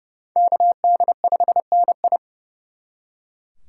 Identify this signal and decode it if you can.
Morse code, heard as a keyed tone, about 37 wpm, 700 Hz. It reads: KD5NI
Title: Speed Racing - USA 2x2 Call Signs 30wpm